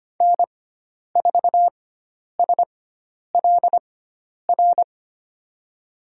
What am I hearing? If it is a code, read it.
Morse code, heard as a keyed tone, about 25 wpm, spaced as for 10 wpm, 700 Hz. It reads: N4SLR
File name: Lesson 16g - Letter c - Call Signs 25wpm